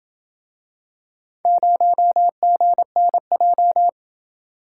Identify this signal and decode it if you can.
Morse code, heard as a keyed tone, about 26 wpm, 700 Hz. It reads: 0GNJ